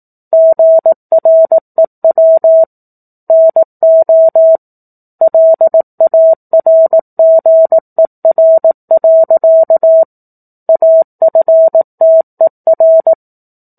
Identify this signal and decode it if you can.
Morse code, heard as a keyed tone, about 18 wpm, 650 Hz. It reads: GREW NO LARGER. AFTER
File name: 04-Chapter-04 18 WPM